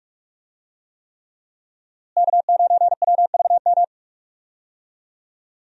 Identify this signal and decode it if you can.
Morse code, heard as a keyed tone, about 46 wpm, 700 Hz. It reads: K9WVM
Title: USA Call Signs - 5 Characters - Repeated 3x - 45wpm